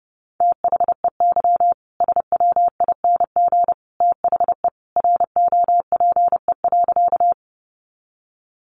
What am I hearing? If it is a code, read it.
Morse code, heard as a keyed tone, about 30 wpm, 700 Hz. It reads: THEY SWING THE ROPE.
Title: Easy Phrases 30wpm